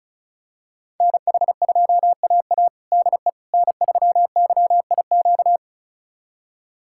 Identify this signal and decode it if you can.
Morse code, heard as a keyed tone, about 36 wpm, 700 Hz. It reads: NH2AA DE N3YIQ
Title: USA Callsign DE Callsign - 5 Characters - 35wpm